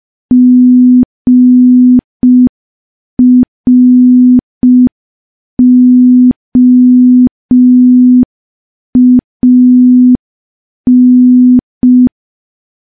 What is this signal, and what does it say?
Morse code, heard as a keyed tone, about 5 wpm, 250 Hz. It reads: GROAN
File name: morse / morse-training-05WPM